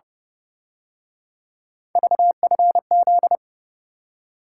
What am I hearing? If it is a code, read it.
Morse code, heard as a keyed tone, about 30 wpm, 700 Hz. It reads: VFZ
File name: Speed-Racing - Three Letter-Number Combinations - 25wpm